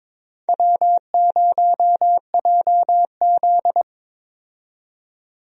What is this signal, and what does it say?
Morse code, heard as a keyed tone, about 22 wpm, 700 Hz. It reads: W0JZ
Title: USA Call Signs - 4 Characters - 22wpm